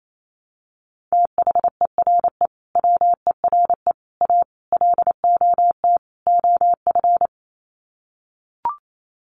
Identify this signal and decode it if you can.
Morse code, heard as a keyed tone, about 28 wpm, 700 Hz. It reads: THERE WERE A LOT OF
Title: Sets of 5 Words 28wpm